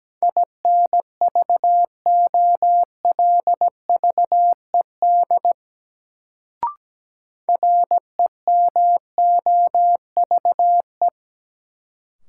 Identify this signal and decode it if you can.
Morse code, heard as a keyed tone, about 17 wpm, 700 Hz. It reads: INVOLVED REMOVE
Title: Sentences from Top 800 Words - New Word Drill - 17wpm